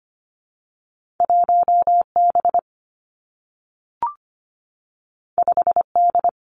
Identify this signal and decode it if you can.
Morse code, heard as a keyed tone, about 25 wpm, 700 Hz. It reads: 1 B 5 D
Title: Two Challenging Letter-Number Combinations 25wpm